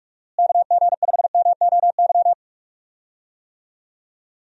Morse code, heard as a keyed tone, about 44 wpm, 700 Hz. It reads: KG5MOY
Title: USA Call Signs - 6 Characters - Repeated 2x - 45wpm